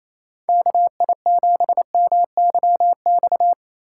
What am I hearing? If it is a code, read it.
Morse code, heard as a keyed tone, about 28 wpm, 700 Hz. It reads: KI7MYX
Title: USA Call Signs - 6 Characters - 4x Character Spacing - 28wpm